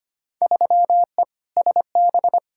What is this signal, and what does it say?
Morse code, heard as a keyed tone, about 25 wpm, 700 Hz. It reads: 3E SB
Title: Field Day Exchanges Encore - 25wpm